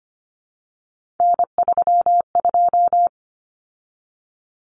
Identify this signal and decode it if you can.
Morse code, heard as a keyed tone, about 25 wpm, 700 Hz. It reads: N32
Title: Common Cut Numbers - 3-Digit 25wpm